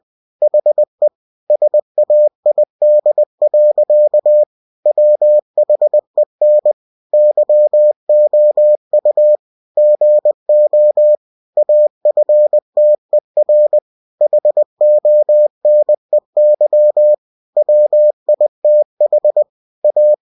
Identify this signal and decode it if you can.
Morse code, heard as a keyed tone, about 20 wpm, 600 Hz. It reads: HE SAID. WHEN YOU GO AFTER HONEY WITH A
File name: cw-books / winnie-the-pooh / winnie-20wpm-page-0005